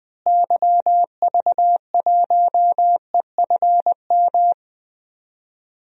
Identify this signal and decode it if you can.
Morse code, heard as a keyed tone, about 20 wpm, 700 Hz. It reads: YV1EFM
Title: DXCC-100 Call Signs - 6 Characters - Repeated 3x - 20wpm